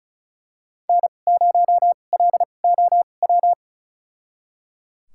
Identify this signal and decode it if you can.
Morse code, heard as a keyed tone, about 36 wpm, spaced as for 25 wpm, 700 Hz. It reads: N0LOW
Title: USA Call Signs - 5 Characters - 2x Character Spacing - 35wpm